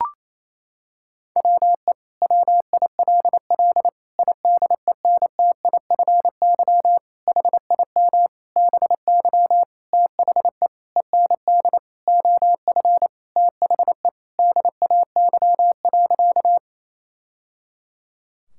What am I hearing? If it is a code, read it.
Morse code, heard as a keyed tone, about 28 wpm, 700 Hz. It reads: WE WILL IDENTIFY HIM BY THE END OF THE DAY.
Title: Sentences from Top 1000 Words - Review Part 6 - 28wpm